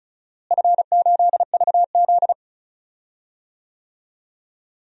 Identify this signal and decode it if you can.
Morse code, heard as a keyed tone, about 36 wpm, 700 Hz. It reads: F8VZ
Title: DXCC-100 Call Signs - 4 Characters - Repeated 2x - 35wpm